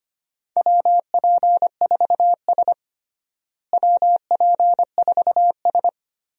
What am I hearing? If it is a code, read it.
Morse code, heard as a keyed tone, about 25 wpm, 700 Hz. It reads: WP4S WP4S
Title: USA Call Signs - 4 Characters - Repeated 3x - 25wpm